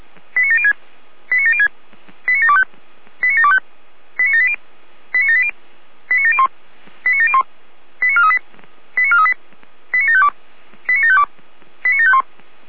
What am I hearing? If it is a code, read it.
Five-tone selective call (ZVEI), heard as quick runs of five tones, tones of about 80 ms. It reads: 89706 89706 89725 89725 89780 89780 89701 89701 89438 89438 89752 89752 89751